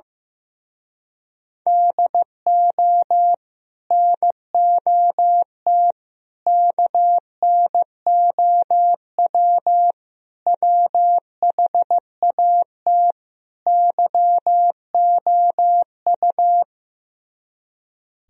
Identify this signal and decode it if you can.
Morse code, heard as a keyed tone, about 15 wpm, 700 Hz. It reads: DO NOT KNOW WHAT YOU
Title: Speed-Racing - Sets of 5 Words 15wpm